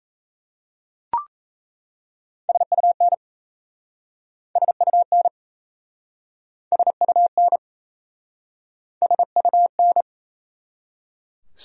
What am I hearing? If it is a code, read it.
Morse code, heard as a keyed tone, about 37 wpm, 700 Hz. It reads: SUN SUN SUN SUN